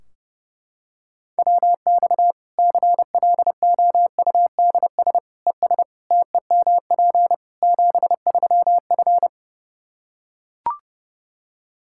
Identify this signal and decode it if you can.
Morse code, heard as a keyed tone, about 30 wpm, 700 Hz. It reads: WX CLOUDS ES TEMP 73F
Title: Practice QSO - Exchange 2 - Info - 30wpm